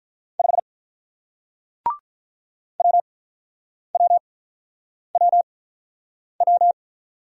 Morse code, heard as a keyed tone, about 48 wpm, spaced as for 9 wpm, 700 Hz. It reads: 5 WWWW